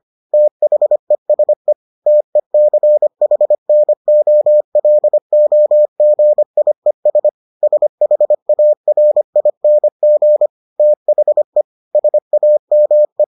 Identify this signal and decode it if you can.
Morse code, heard as a keyed tone, about 25 wpm, 600 Hz. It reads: THESE TECHNOLOGIES SHARING THE SAME